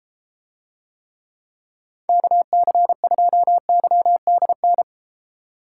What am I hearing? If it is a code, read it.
Morse code, heard as a keyed tone, about 32 wpm, 700 Hz. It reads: KC2YDN